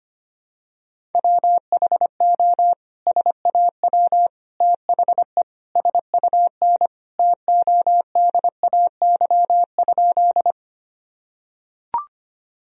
Morse code, heard as a keyed tone, about 25 wpm, 700 Hz. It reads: WHO SAW THE SUN TODAY?
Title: Sentences from Top 1000 Words - Review Part 1 - 25wpm